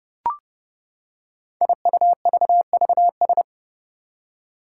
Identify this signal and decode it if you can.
Morse code, heard as a keyed tone, about 30 wpm, 700 Hz. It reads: IUVVS
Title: Five Challenging Letter-Number Combinations 30wpm